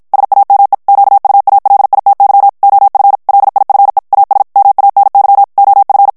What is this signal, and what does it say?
Morse code, heard as a keyed tone, about 53 wpm, 800 Hz. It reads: SAME QUALITY OF LIFE AS MANY OF
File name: morse mp3 / ANGLAIS texte clair 1 vitesse 50 mots minute